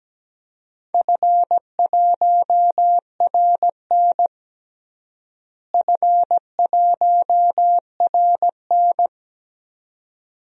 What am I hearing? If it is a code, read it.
Morse code, heard as a keyed tone, about 17 wpm, 700 Hz. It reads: F1RN F1RN
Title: DXCC-100 Call Signs - 4 Characters - Repeated 2x - 17wpm